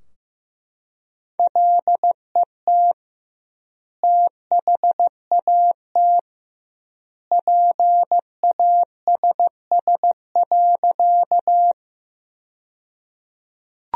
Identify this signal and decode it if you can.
Morse code, heard as a keyed tone, about 16 wpm, 700 Hz. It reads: LET THAT PASS.
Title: 2x Word-Spacing - Sentences from Top 400 Words - 15wpm